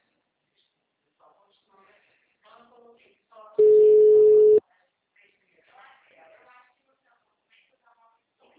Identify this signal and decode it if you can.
Ringback tone (Europe, 425 Hz), with speech.